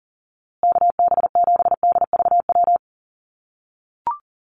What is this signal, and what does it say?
Morse code, heard as a keyed tone, about 40 wpm, 700 Hz. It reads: KB7DVW
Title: USA Call Signs Repeated 3x 40wpm